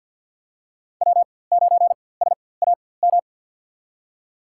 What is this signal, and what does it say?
Morse code, heard as a keyed tone, about 53 wpm, spaced as for 24 wpm, 700 Hz. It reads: W9SAM
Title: USA Call Signs - 4x Character Spacing - 50wpm